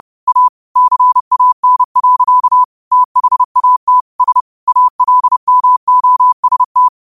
Morse code, heard as a keyed tone, about 30 wpm, 1000 Hz. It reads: A CAN. THAT S ALMOST
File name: qotdfiles / 30wpm